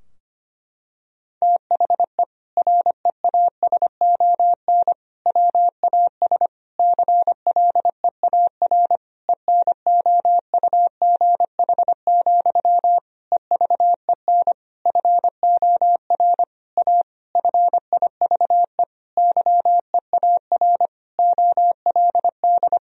Morse code, heard as a keyed tone, about 25 wpm, 700 Hz. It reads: THE REASON WAS CLEAR ENOUGH, EVEN FOR A FIVE YEAR OLD
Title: Sentences from Top 1000 Words - Review Part 2 - 25wpm